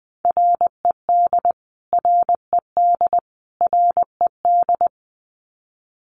Morse code, heard as a keyed tone, about 20 wpm, 700 Hz. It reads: RED RED RED
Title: Top 500 Words Repeated 20wpm